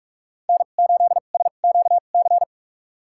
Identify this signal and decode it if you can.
Morse code, heard as a keyed tone, about 46 wpm, spaced as for 33 wpm, 700 Hz. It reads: N8SQC
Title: USA Call Signs - 2x Character Spacing - 45wpm